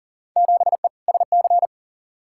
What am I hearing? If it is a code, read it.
Morse code, heard as a keyed tone, about 40 wpm, 700 Hz. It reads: 7E SC
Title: Field Day Exchanges Encore - 40wpm